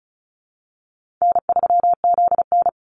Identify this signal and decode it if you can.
Morse code, heard as a keyed tone, about 34 wpm, 700 Hz. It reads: N3ZN
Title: Call Signs Repeated 2x - 35wpm